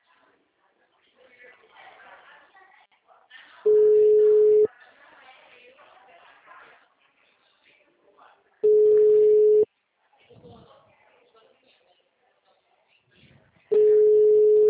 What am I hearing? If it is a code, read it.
Ringback tone (Europe, 425 Hz), with speech.